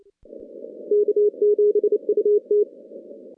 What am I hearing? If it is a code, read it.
Morse code, heard as a keyed tone, about 30 wpm, 410 Hz. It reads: K7UT
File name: contests / CW-SS-k5tr- -2200